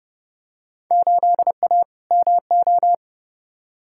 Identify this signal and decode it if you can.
Morse code, heard as a keyed tone, about 30 wpm, 700 Hz. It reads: 8A MO
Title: Field Day Exchanges Encore - 30wpm